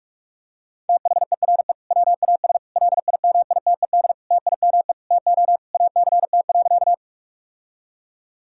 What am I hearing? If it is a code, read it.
Morse code, heard as a keyed tone, about 46 wpm, 700 Hz. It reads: THERE WAS LIMITED TIME TO ACT.
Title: Sentences from Top 1000 Words - Review Part 6 - 45wpm